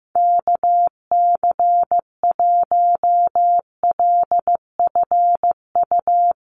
Morse code, heard as a keyed tone, about 15 wpm, 700 Hz. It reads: KC1LFU
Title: USA Call Signs Repeated 2x 15wpm